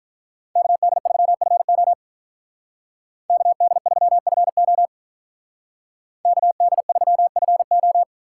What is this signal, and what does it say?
Morse code, heard as a keyed tone, about 44 wpm, 700 Hz. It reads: KD3FO KD3FO KD3FO
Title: Speed Racing - USA 2x2 Call Signs 35wpm